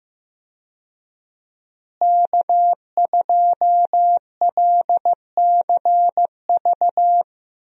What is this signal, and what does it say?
Morse code, heard as a keyed tone, about 15 wpm, 700 Hz. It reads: K2LCV
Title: USA Call Signs - 5 Characters - Repeated 2x - 15wpm